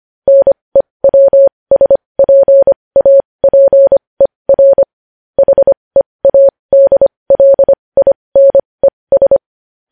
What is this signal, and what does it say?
Morse code, heard as a keyed tone, about 25 wpm, spaced as for 20 wpm, 550 Hz. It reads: NEWSPAPER HEADLINES
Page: News Headlines in Morse Code at 20 WPM